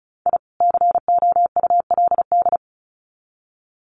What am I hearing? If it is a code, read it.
Morse code, heard as a keyed tone, about 34 wpm, 700 Hz. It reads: I COULD
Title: Sets of 2 Words 35wpm